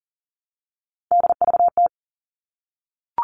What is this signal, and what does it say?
Morse code, heard as a keyed tone, about 40 wpm, 700 Hz. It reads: DVT
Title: All Cut Numbers - 3-Digit 40wpm